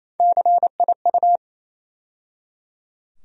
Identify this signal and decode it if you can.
Morse code, heard as a keyed tone, about 28 wpm, 700 Hz. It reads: CIU
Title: USA Call Sign Postfixes - 3 Letters 28wpm